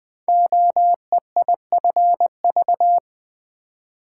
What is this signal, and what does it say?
Morse code, heard as a keyed tone, about 20 wpm, 700 Hz. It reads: OEIFV
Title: Code Groups - 5 Letters - 20wpm